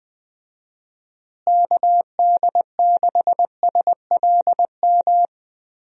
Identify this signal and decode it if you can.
Morse code, heard as a keyed tone, about 20 wpm, 700 Hz. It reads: KD6SLM
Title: USA Call Signs - 6 Characters - Repeated 2x - 20wpm